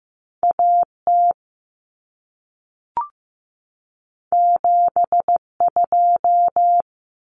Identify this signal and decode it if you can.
Morse code, heard as a keyed tone, about 15 wpm, 700 Hz. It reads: A T 7 2
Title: Common Cut Numbers - 2-Digit 15wpm